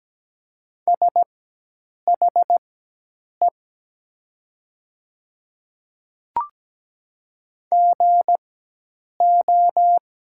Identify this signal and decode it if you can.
Morse code, heard as a keyed tone, about 17 wpm, spaced as for 8 wpm, 700 Hz. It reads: SHE GO
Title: 4x Character Spacing - Top 200 Words - 17wpm